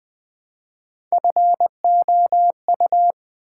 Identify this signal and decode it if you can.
Morse code, heard as a keyed tone, about 20 wpm, 700 Hz. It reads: FOU
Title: Build A Word - 1 Letter At A Time - 4-Letter Words - 20wpm